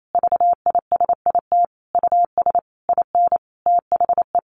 Morse code, heard as a keyed tone, about 28 wpm, 700 Hz. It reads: VISIT US IN THE
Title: Sentences from Top 1400 Words 28wpm